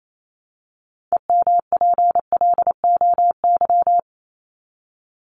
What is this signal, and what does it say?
Morse code, heard as a keyed tone, about 28 wpm, 700 Hz. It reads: EMPLOY
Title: ED Suffix - Top 100 Words with Base Words 28wpm